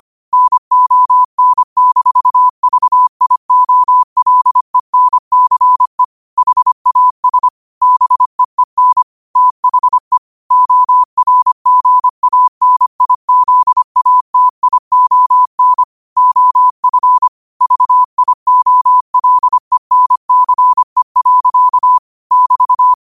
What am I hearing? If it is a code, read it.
Morse code, heard as a keyed tone, about 25 wpm, 1000 Hz. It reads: NON-VIOLENCE HAS BEEN THE ORGANIZATION OF VIOLENCE. <BT>
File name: qotdfiles / 25wpm